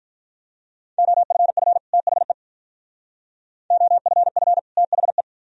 Morse code, heard as a keyed tone, about 44 wpm, 700 Hz. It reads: OFF THE OFF THE